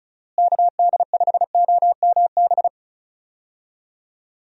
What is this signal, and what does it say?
Morse code, heard as a keyed tone, about 34 wpm, 700 Hz. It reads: KD5OMB
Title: USA Call Signs - 6 Characters - Repeated 3x - 35wpm